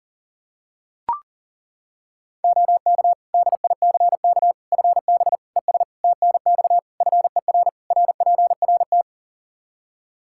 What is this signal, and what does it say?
Morse code, heard as a keyed tone, about 40 wpm, 700 Hz. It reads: OK DICK FB ES TNX FER RPRT
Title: Practice QSO - Exchange 2 - Info - 40wpm